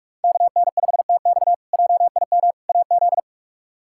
Morse code, heard as a keyed tone, about 46 wpm, 700 Hz. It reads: KN5TX JIM AZ
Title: K1USN SST QSO Practice - 45wpm